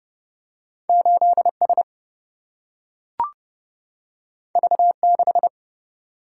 Morse code, heard as a keyed tone, about 30 wpm, 700 Hz. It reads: 8 S V 6
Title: Single-Letter Single-Number 30wpm